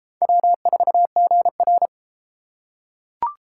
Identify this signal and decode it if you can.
Morse code, heard as a keyed tone, about 31 wpm, 700 Hz. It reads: W4GR